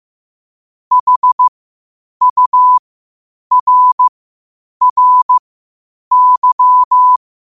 Morse code, heard as a keyed tone, about 15 wpm, spaced as for 8 wpm, 1000 Hz. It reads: HURRY
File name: qotdfiles / 8wpm